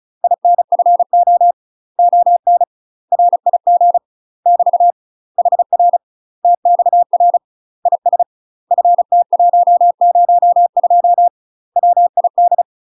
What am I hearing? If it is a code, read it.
Morse code, heard as a keyed tone, about 35 wpm, 700 Hz. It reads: INFO ON RIG <BT> HR TXR IS FT102 WID
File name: UK MASTER QSOs TRAINING / 35WPM / UK MASTER QSOs 0034